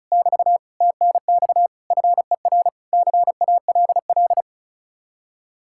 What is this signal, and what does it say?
Morse code, heard as a keyed tone, about 46 wpm, 700 Hz. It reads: <BT> TNX FER CALL